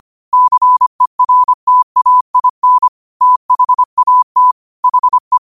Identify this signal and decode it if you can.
Morse code, heard as a keyed tone, about 25 wpm, 1000 Hz. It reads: CERTAIN THAT HE
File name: qotdfiles / 25wpm